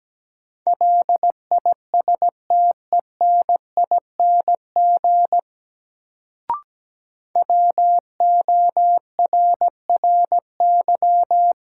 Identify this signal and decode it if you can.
Morse code, heard as a keyed tone, about 17 wpm, 700 Hz. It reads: LISTENING WORRY
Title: Sets of 5 Words - New Word Drill - 17wpm